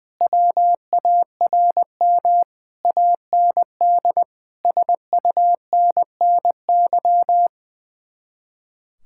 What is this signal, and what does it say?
Morse code, heard as a keyed tone, about 20 wpm, 700 Hz. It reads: WARM AND SUNNY
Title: Speed-Racing Common QSO Phrases - 20wpm